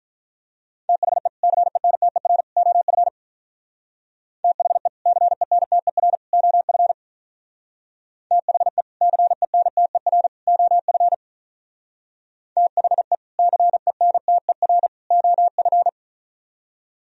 Morse code, heard as a keyed tone, about 42 wpm, 690 Hz. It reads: THE CENTER OF THE CENTER OF THE CENTER OF THE CENTER OF
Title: Speed-Racing - Sets of 3 Words 35wpm